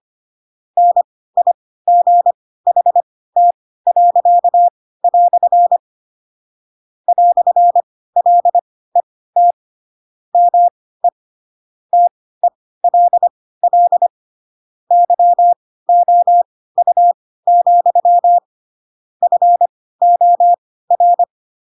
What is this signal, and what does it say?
Morse code, heard as a keyed tone, about 25 wpm, spaced as for 16 wpm, 700 Hz. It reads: NIGHT." "LET ME TELL YOU, FOR
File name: Pinocchio / Pinocchio ch4 10 25 ik1hge